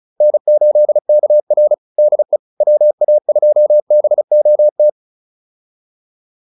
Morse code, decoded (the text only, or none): N8KR DE WA2BOT